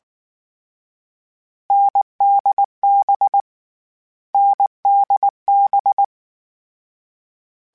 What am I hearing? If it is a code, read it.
Morse code, heard as a keyed tone, about 19 wpm, 800 Hz. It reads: NDB NDB
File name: Short Runs / Short Runs-mp3s / NDB NDB6 @18